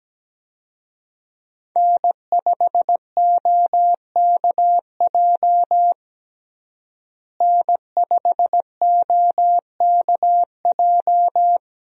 Morse code, heard as a keyed tone, about 17 wpm, 700 Hz. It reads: N5OKJ N5OKJ
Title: USA Call Signs - 5 Characters - Repeated 2x - 17wpm